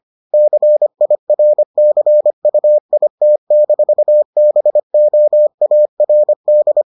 Morse code, heard as a keyed tone, about 25 wpm, 600 Hz. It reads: CIRCUIT-BOARD